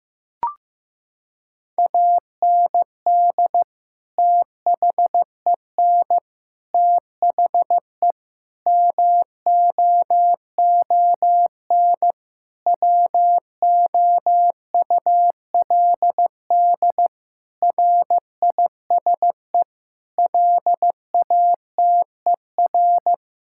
Morse code, heard as a keyed tone, about 15 wpm, 700 Hz. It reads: AND THEN THE MOON WOULD RISE LATER
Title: Sentences from Top 1000 Words - Review Part 6 - 15wpm